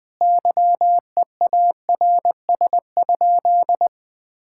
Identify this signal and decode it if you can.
Morse code, heard as a keyed tone, about 20 wpm, 700 Hz. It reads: YEARS?
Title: Sentences from Top 1000 Words - Review Part 6 - 20wpm